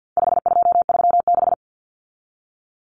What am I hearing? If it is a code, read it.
Morse code, heard as a keyed tone, about 50 wpm, 700 Hz. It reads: 5236
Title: Numbers - 4-Digit 50wpm